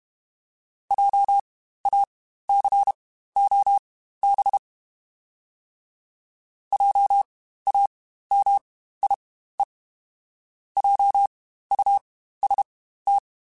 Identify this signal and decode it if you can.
Morse code, heard as a keyed tone, about 32 wpm, spaced as for 15 wpm, 800 Hz. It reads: JACOB JAMIE JUST
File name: Spaced Alphas / Spaced Alphas mp3s / Js sp @30